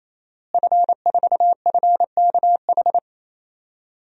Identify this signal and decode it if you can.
Morse code, heard as a keyed tone, about 28 wpm, 700 Hz. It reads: F4FKH
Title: DXCC-100 Call Signs - 5 Characters - Repeated 3x - 28wpm